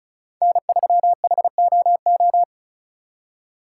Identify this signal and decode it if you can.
Morse code, heard as a keyed tone, about 34 wpm, 700 Hz. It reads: N3HOO
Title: USA Call Signs - 5 Characters - 3x Character Spacing - 35wpm